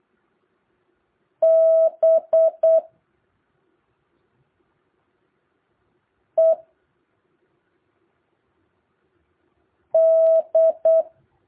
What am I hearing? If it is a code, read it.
Morse code, heard as a keyed tone, about 8 wpm, 650 Hz. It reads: B E D
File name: jh8cbh / 2023cw